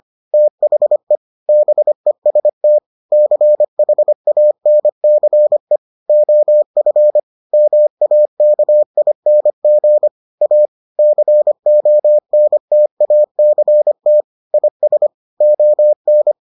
Morse code, heard as a keyed tone, about 25 wpm, 600 Hz. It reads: THE BEST CHANCE OF MAKING A CONTACT IS ON